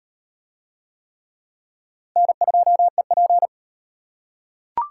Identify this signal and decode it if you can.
Morse code, heard as a keyed tone, about 38 wpm, 700 Hz. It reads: N2EP